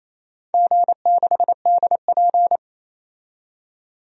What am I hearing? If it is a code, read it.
Morse code, heard as a keyed tone, about 28 wpm, 700 Hz. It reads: G6DP
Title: DXCC-100 Call Signs - 4 Characters - 28wpm